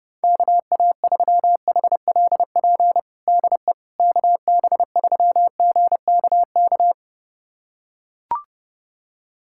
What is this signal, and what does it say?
Morse code, heard as a keyed tone, about 30 wpm, 690 Hz. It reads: KA3HLP DE KB3GKK
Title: USA Callsign DE Callsign 30wpm